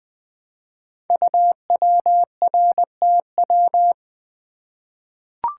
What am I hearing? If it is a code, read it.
Morse code, heard as a keyed tone, about 20 wpm, 700 Hz. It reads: UWRTW
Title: Code Groups - 5 Letters - 20wpm